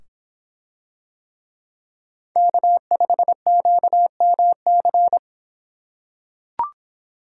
Morse code, heard as a keyed tone, about 25 wpm, 700 Hz. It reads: K5QMC